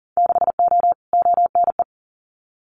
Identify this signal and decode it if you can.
Morse code, heard as a keyed tone, about 41 wpm, 700 Hz. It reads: 6O ONE
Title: Winter Field Day Exchanges Encore 40wpm